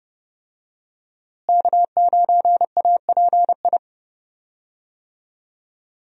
Morse code, heard as a keyed tone, about 31 wpm, 700 Hz. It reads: K9API
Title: USA Call Signs - 5 Characters - Repeated 3x - 30wpm